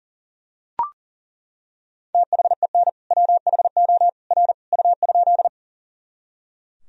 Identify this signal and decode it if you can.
Morse code, heard as a keyed tone, about 40 wpm, 700 Hz. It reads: THEN WHO R U?
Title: Sentences from Top 100 Words and Abbreviations 40wpm